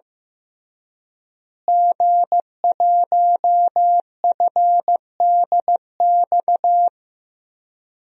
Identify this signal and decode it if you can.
Morse code, heard as a keyed tone, about 15 wpm, 700 Hz. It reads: G1FDX
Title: DXCC-100 Call Signs - 5 Characters - Repeated 3x - 15wpm